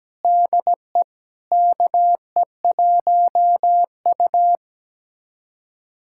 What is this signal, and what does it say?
Morse code, heard as a keyed tone, about 17 wpm, 700 Hz. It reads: DE KE1U
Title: USA Callsign DE Callsign - 4 Characters - 17wpm